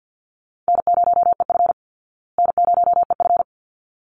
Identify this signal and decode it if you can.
Morse code, heard as a keyed tone, about 48 wpm, 700 Hz. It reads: N0EF N0EF
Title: Easy Call Signs Repeated 2x - 50wpm